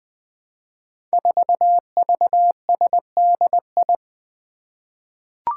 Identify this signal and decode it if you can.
Morse code, heard as a keyed tone, about 20 wpm, 700 Hz. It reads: 4VSDI
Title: Five Challenging Letter-Number Combinations 20wpm